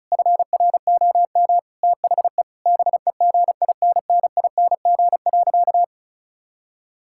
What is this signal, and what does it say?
Morse code, heard as a keyed tone, about 34 wpm, 700 Hz. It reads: FROM THE BEGINNING.
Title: Sentences from Top 1000 Words - Review Part 5 - 35wpm